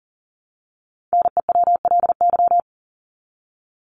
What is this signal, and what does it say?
Morse code, heard as a keyed tone, about 37 wpm, 700 Hz. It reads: NEWLY
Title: LY Suffix - Top 100 Words with Base Words 40wpm